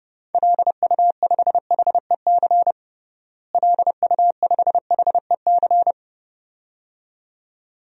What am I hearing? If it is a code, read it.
Morse code, heard as a keyed tone, about 32 wpm, 700 Hz. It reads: LU5HEC LU5HEC
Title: DXCC-100 Call Signs - 6 Characters - Repeated 2x - 30wpm